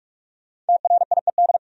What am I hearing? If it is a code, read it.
Morse code, heard as a keyed tone, about 61 wpm, spaced as for 53 wpm, 700 Hz. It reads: TRIED
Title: Speed-Racing - Sets of 2 Words 45wpm